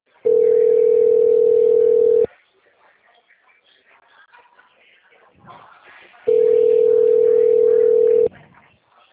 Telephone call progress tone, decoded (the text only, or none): ringback tone